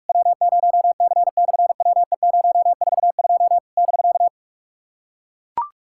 Morse code, heard as a keyed tone, about 44 wpm, 700 Hz. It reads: W0C/WE042 <BK>
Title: SOTA QSO Practice 45wpm